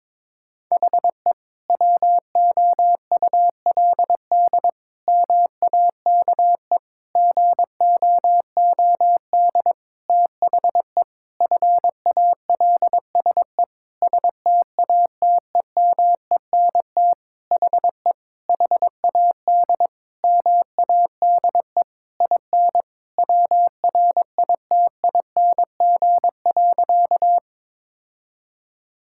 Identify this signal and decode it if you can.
Morse code, heard as a keyed tone, about 22 wpm, 700 Hz. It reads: HE WOULD MAKE GOOD THE FALSE STATEMENT HE HAD MADE IN WRITING.
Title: Sentences from Top 1600 Words 22wpm